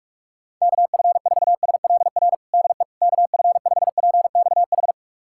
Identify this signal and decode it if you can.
Morse code, heard as a keyed tone, about 46 wpm, 700 Hz. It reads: KF4SLR DE KF5PXH
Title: USA Callsign DE Callsign - 6 Characters - 45wpm